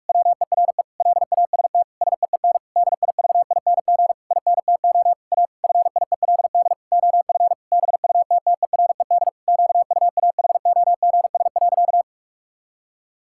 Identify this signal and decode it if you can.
Morse code, heard as a keyed tone, about 46 wpm, 700 Hz. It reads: WERE LAST SEEN DIVING INTO A FIELD OF BUTTERED QUAHOGS.
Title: Pangram Challenge v2 - 45wpm